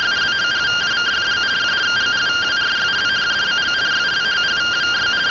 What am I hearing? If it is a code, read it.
Two-tone teleprinter signal, about 170 Hz shift.